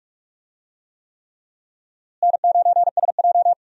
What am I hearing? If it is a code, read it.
Morse code, heard as a keyed tone, about 46 wpm, 700 Hz. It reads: N9SJ
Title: USA Call Signs - 4 Characters - Repeated 3x - 45wpm